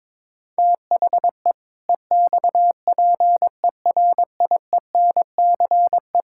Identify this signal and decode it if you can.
Morse code, heard as a keyed tone, about 22 wpm, 700 Hz. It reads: THE EXPERIENCE
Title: Sentences from Top 1000 Words - Review Part 5 - 22wpm